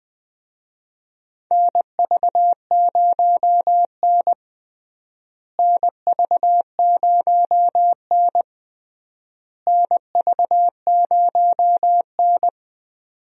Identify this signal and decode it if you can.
Morse code, heard as a keyed tone, about 20 wpm, 700 Hz. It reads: NV0N NV0N NV0N
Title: USA Call Signs - 4 Characters - Repeated 3x - 20wpm